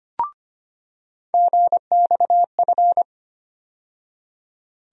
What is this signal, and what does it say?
Morse code, heard as a keyed tone, about 25 wpm, 700 Hz. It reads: GXF
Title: USA Call Sign Postfixes - 3 Letters 25wpm